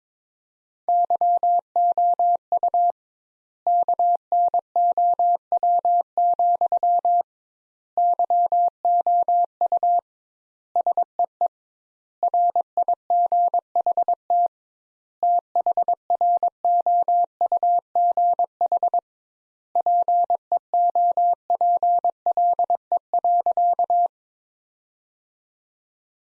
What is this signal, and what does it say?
Morse code, heard as a keyed tone, about 22 wpm, 700 Hz. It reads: YOU KNOW, YOU SEE RIGHT THROUGH PEOPLE.
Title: Sentences Top 200 Words - 22wpm